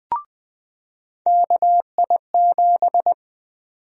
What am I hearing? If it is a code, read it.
Morse code, heard as a keyed tone, about 20 wpm, 700 Hz. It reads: KI7
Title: USA Call Sign Prefixes - 20wpm